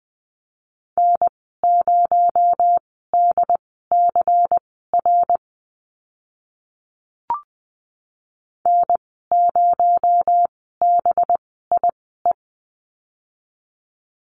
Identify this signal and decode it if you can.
Morse code, heard as a keyed tone, about 20 wpm, spaced as for 14 wpm, 700 Hz. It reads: N0DCR N0BIE
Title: 238 Lesson 38g - Letter q - Call Signs 20wpm